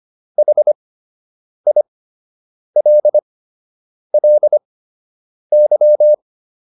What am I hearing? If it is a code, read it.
Morse code, heard as a keyed tone, about 25 wpm, spaced as for 8 wpm, 600 Hz. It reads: HILLY